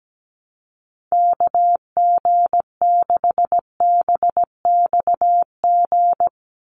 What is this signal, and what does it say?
Morse code, heard as a keyed tone, about 17 wpm, 700 Hz. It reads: KG6BXG
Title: USA Call Signs 17wpm